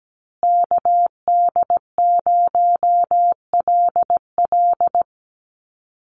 Morse code, heard as a keyed tone, about 17 wpm, 700 Hz. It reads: KD0LL
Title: USA Call Signs Repeated 3x 17wpm